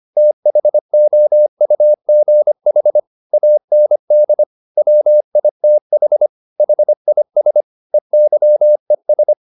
Morse code, heard as a keyed tone, about 25 wpm, 600 Hz. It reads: THOUGH AND WITH HIS EYES
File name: the old man and the sea part 1 25wpm